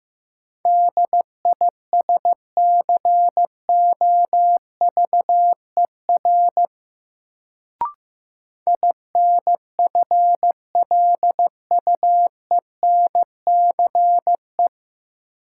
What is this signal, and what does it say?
Morse code, heard as a keyed tone, about 15 wpm, 700 Hz. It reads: DISCOVER INFLUENCE
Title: Sentences from Top 1100 Words - New Word Drill - 15wpm